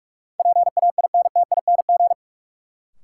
Morse code, heard as a keyed tone, about 44 wpm, 700 Hz. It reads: PAINTING
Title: Sentences from Top 1000 Words - Review Part 6 - New Word Drill - 45wpm